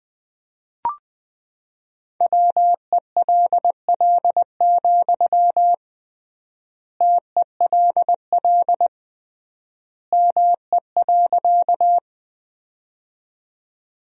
Morse code, heard as a keyed tone, about 20 wpm, 700 Hz. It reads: WELL, TELL ME.
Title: 3x Word-Spacing - Sentences from Top 200 Words - 20wpm